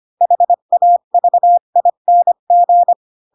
Morse code, heard as a keyed tone, about 25 wpm, 700 Hz. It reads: HAVING
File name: Pinocchio / Pinocchio ch2 20 25 ik1hge